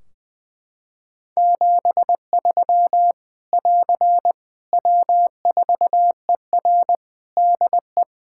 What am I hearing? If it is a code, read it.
Morse code, heard as a keyed tone, about 21 wpm, 700 Hz. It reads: 73 <AR> W4ER DE
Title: K1USN SST QSO Practice 20wpm